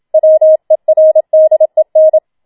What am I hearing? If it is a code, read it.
Morse code, heard as a keyed tone, about 27 wpm, 630 Hz. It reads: WERDEN
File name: Schneewittchen Teil2 27wpm